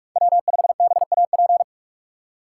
Morse code, heard as a keyed tone, about 46 wpm, 700 Hz. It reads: W5BAP
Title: USA Call Signs - 5 Characters - 3x Character Spacing - 45wpm